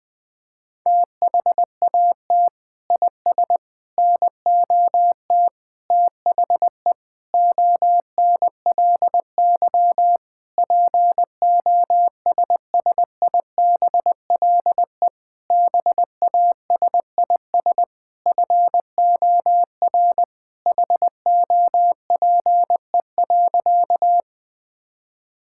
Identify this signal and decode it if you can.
Morse code, heard as a keyed tone, about 21 wpm, 700 Hz. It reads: THAT IS NOT THE ONLY POSSIBLE BASIS FOR HOPE.
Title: Sentences from Top 1000 Words - Review Part 5 - 20wpm